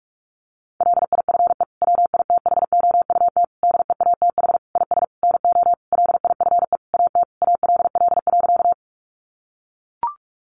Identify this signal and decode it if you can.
Morse code, heard as a keyed tone, about 44 wpm, 700 Hz. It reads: LIFE WITHOUT DEATH IS NO LIFE AT ALL.
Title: Sentences from Top 1100 Words 45wpm